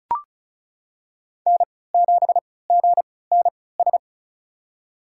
Morse code, heard as a keyed tone, about 36 wpm, spaced as for 20 wpm, 700 Hz. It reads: N7GNS
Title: USA Call Signs - 5 Characters - 3x Character Spacing - 35wpm